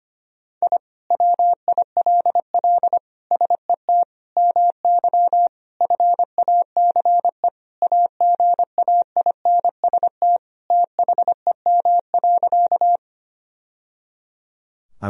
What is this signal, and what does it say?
Morse code, heard as a keyed tone, about 25 wpm, 700 Hz. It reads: I WILL SET MY FACE AGAINST THEM.
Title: Sentences from Top 1000 Words - Review Part 2 - 25wpm